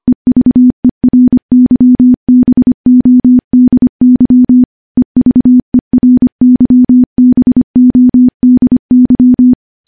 Morse code, heard as a keyed tone, about 25 wpm, 250 Hz. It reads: EVERYBODY EVERYBODY
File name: morse / morse-training-25WPM